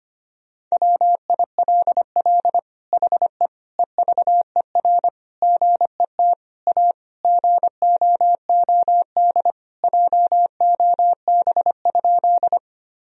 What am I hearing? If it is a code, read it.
Morse code, heard as a keyed tone, about 25 wpm, 700 Hz. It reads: WILL HE EVER GET A GOOD JOB?
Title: Sentences from Top 1000 Words - Review Part 3 - 25wpm